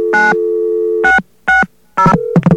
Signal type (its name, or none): DTMF